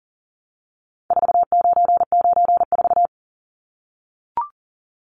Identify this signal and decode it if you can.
Morse code, heard as a keyed tone, about 40 wpm, 700 Hz. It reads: 4994
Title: Numbers - 4-Digit 40wpm